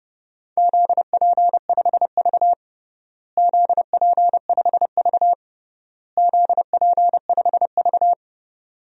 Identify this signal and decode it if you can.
Morse code, heard as a keyed tone, about 30 wpm, 700 Hz. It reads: ZP5V ZP5V ZP5V
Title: DXCC-100 Call Signs - 4 Characters - Repeated 3x - 30wpm